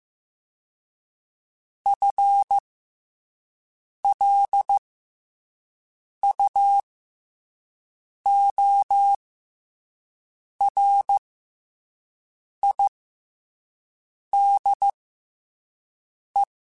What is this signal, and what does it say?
Morse code, heard as a keyed tone, about 15 wpm, spaced as for 5 wpm, 800 Hz. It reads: FLUORIDE
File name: FL Runs / FL Runs mp3s / FL Runs @5